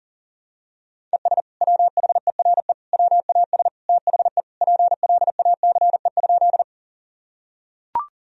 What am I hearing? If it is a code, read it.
Morse code, heard as a keyed tone, about 40 wpm, 700 Hz. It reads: ES WHERE WAS THE PLACE?
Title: Sentences from Top 200 Words and Abbreviations 40wpm